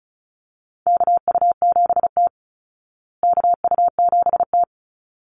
Morse code, heard as a keyed tone, about 34 wpm, 700 Hz. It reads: KU7T KU7T
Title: Call Signs Repeated 3x - 35wpm